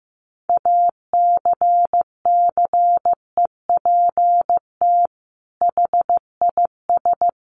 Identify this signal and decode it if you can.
Morse code, heard as a keyed tone, about 15 wpm, 700 Hz. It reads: ACCEPT HIS
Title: Sentences from Top 1300 Words 15wpm